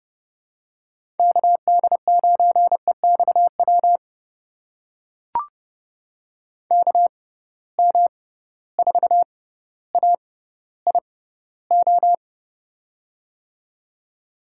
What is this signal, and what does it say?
Morse code, heard as a keyed tone, about 31 wpm, spaced as for 16 wpm, 700 Hz. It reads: KD9EXW KM4AIO
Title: USA Call Signs - 6 Characters - 6x Character Spacing - 30wpm